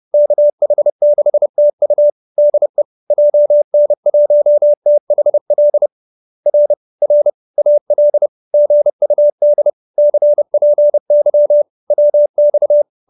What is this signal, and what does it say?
Morse code, heard as a keyed tone, about 30 wpm, 580 Hz. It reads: KH6TU DE JN1THL R R AL GUD CPY WX
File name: QSO309 30